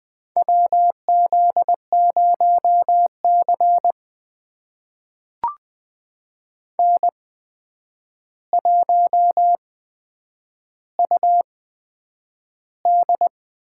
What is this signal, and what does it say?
Morse code, heard as a keyed tone, about 20 wpm, spaced as for 9 wpm, 700 Hz. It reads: WZ0C N1UD
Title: USA Call Signs - 8x Character Spacing - 20wpm